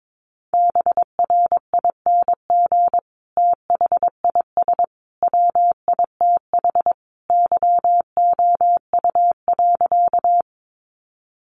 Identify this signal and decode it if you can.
Morse code, heard as a keyed tone, about 22 wpm, 700 Hz. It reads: BRING THIS WITH YOU.
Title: Sentences from Top 500 Words 22wpm